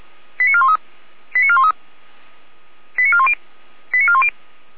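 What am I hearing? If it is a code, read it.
Five-tone selective call (ZVEI), heard as quick runs of five tones, tones of about 70 ms. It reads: 89413 89413 89410 89410
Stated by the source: Five-tone ZVEI selective call received by SDR.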